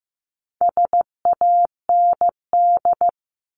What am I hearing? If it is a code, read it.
Morse code, heard as a keyed tone, about 15 wpm, 700 Hz. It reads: SAND
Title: Top 1501-1750 Words 15wpm